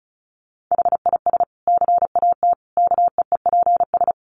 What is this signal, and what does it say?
Morse code, heard as a keyed tone, about 34 wpm, 700 Hz. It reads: HIS CAT KEEPS
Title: Additional Easy Phrases 35wpm